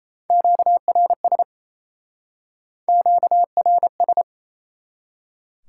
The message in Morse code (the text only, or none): QRS QRS